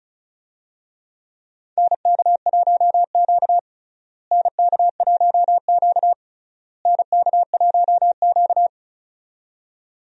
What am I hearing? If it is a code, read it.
Morse code, heard as a keyed tone, about 36 wpm, 700 Hz. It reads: NK1Q NK1Q NK1Q
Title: USA Call Signs - 4 Characters - Repeated 3x - 35wpm